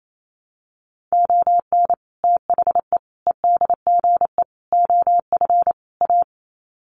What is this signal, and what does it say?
Morse code, heard as a keyed tone, about 28 wpm, 700 Hz. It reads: ON THE EDGE OF A
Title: Sets of 5 Words Encore2 28wpm